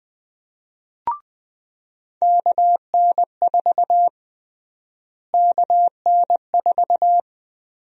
Morse code, heard as a keyed tone, about 20 wpm, 700 Hz. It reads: KN4 KN4
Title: USA Call Sign Prefixes - Repeated 2x - 20wpm